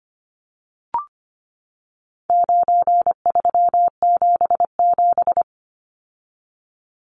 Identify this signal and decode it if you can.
Morse code, heard as a keyed tone, about 25 wpm, 700 Hz. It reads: 9377
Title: Numbers - 4-Digit 25wpm